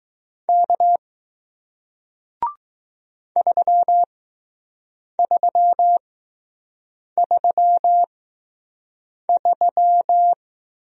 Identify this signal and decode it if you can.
Morse code, heard as a keyed tone, about 20 wpm, spaced as for 6 wpm, 700 Hz. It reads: K 3333